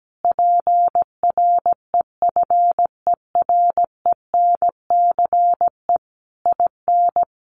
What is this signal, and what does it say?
Morse code, heard as a keyed tone, about 17 wpm, 700 Hz. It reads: PREFERENCE IN
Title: Sentences from Top 1500 Words 17wpm